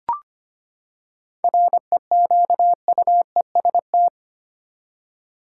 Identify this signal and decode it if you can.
Morse code, heard as a keyed tone, about 25 wpm, 700 Hz. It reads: REQUEST
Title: Seven Letter Words - 25wpm